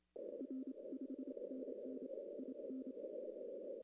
Morse code, heard as a keyed tone, about 27 wpm, 260 Hz. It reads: R 5NN IN
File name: contests / arrl-160-k5tr-op-w5tn -0300